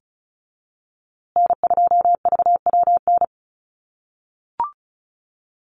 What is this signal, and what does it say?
Morse code, heard as a keyed tone, about 34 wpm, 700 Hz. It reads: N2VWN
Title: USA Call Signs Repeated 3x 35wpm